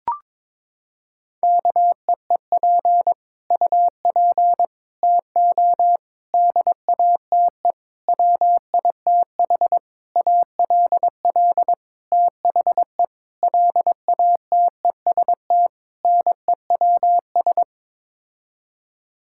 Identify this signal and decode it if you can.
Morse code, heard as a keyed tone, about 22 wpm, 700 Hz. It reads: KEEP UP TO DATE WITH ALL THE LATEST NEWS
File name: Sets of 9 Words - 22wpm